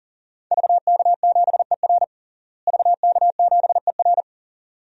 Morse code, heard as a keyed tone, about 40 wpm, 700 Hz. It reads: VK7ER VK7ER
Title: DXCC-100 Call Signs - 5 Characters - Repeated 2x - 40wpm